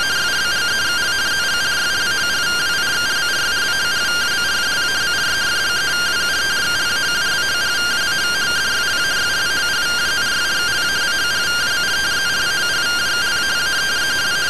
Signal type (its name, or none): two-tone teleprinter signal